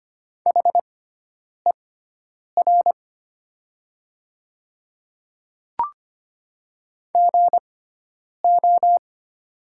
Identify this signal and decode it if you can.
Morse code, heard as a keyed tone, about 25 wpm, spaced as for 9 wpm, 700 Hz. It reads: HER GO
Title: 6x Character Spacing - Top 100 Words - 25wpm